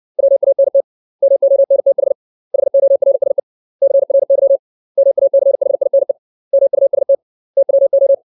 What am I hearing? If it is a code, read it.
Morse code, heard as a keyed tone, about 60 wpm, 550 Hz. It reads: WANT MONTH HORSE CRY MACHINE MUST TWO
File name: 632 words / 632 Words 60WPM